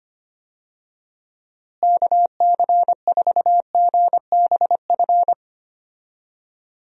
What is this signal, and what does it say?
Morse code, heard as a keyed tone, about 25 wpm, 700 Hz. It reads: KC4GBF
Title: USA Call Signs - 6 Characters - Repeated 3x - 25wpm